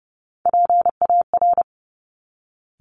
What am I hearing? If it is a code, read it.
Morse code, heard as a keyed tone, about 30 wpm, 700 Hz. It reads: PAR
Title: Top Three Letter Combinations in Words 30wpm